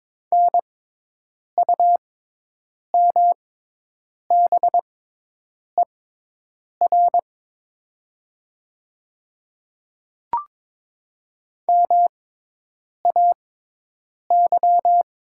Morse code, heard as a keyed tone, about 23 wpm, spaced as for 8 wpm, 700 Hz. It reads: NUMBER MAY
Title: 6x Character Spacing - Top 100 Words - 22wpm